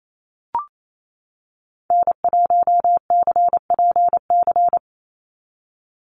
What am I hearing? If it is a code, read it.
Morse code, heard as a keyed tone, about 28 wpm, 700 Hz. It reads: N1CPC
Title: USA Call Signs 28wpm